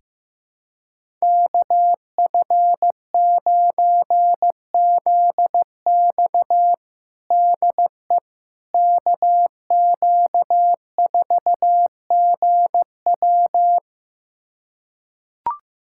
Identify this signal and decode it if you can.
Morse code, heard as a keyed tone, about 15 wpm, 700 Hz. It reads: KF9ZX DE KQ4GW
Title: USA Callsign DE Callsign - 5 Characters - 15wpm